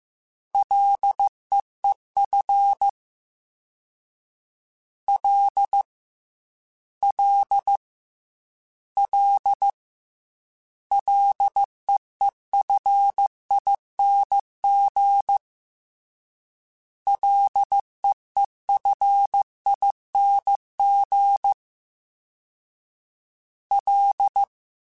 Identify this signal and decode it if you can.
Morse code, heard as a keyed tone, about 15 wpm, 800 Hz. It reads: LEEF L L L LEEFING LEEFING L
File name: Ls / Single L m3s / Single L Prefixe 3x2 Twice @13